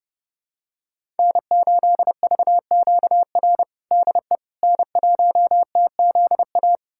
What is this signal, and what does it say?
Morse code, heard as a keyed tone, about 30 wpm, 700 Hz. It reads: N8VQR DE N1TZA
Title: USA Callsign DE Callsign - 5 Characters - 30wpm